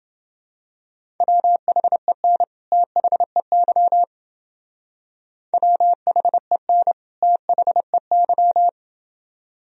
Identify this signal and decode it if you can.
Morse code, heard as a keyed tone, about 30 wpm, 700 Hz. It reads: WHEN THEY WHEN THEY